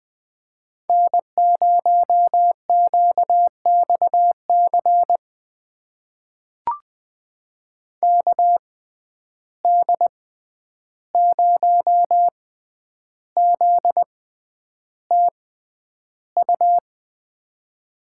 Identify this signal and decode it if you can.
Morse code, heard as a keyed tone, about 20 wpm, spaced as for 10 wpm, 700 Hz. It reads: N0QXC KD0ZTU
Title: USA Call Signs - 6x Character Spacing - 20wpm